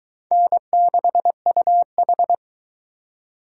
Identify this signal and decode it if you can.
Morse code, heard as a keyed tone, about 23 wpm, 700 Hz. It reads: N6UH